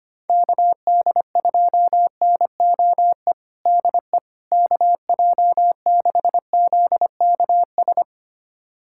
Morse code, heard as a keyed tone, about 25 wpm, 700 Hz. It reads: KD2NOE DE KJ6ZKS
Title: USA Callsign DE Callsign - 6 Characters - 25wpm